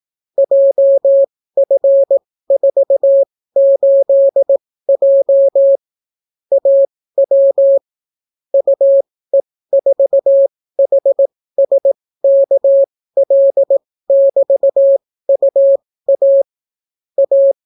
Morse code, heard as a keyed tone, about 18 wpm, spaced as for 14 wpm, 550 Hz. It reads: JF48J AW UE4HSKL<BT>UA A